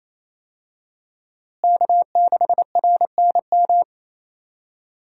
Morse code, heard as a keyed tone, about 27 wpm, 700 Hz. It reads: K6RNM
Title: USA Call Signs - 5 Characters - Repeated 3x - 28wpm